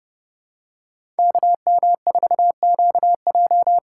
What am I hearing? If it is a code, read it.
Morse code, heard as a keyed tone, about 30 wpm, 700 Hz. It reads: KM4QJ
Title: USA Call Signs - 6 Characters - 6x Character Spacing - 30wpm